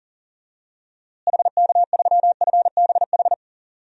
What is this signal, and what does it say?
Morse code, heard as a keyed tone, about 40 wpm, 700 Hz. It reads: HK3FBH
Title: DXCC-100 Call Signs - 6 Characters - Repeated 2x - 40wpm